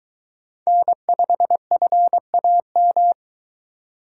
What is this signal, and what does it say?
Morse code, heard as a keyed tone, about 22 wpm, 700 Hz. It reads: N5FAM